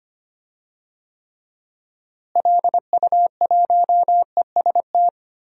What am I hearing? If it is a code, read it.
Morse code, heard as a keyed tone, about 25 wpm, 700 Hz. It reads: LU1EST
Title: DXCC-100 Call Signs - 6 Characters - Repeated 3x - 25wpm